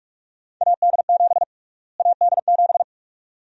Morse code, heard as a keyed tone, about 46 wpm, 700 Hz. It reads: AD7 AD7
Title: USA Call Sign Prefixes - Repeated 2x - 45wpm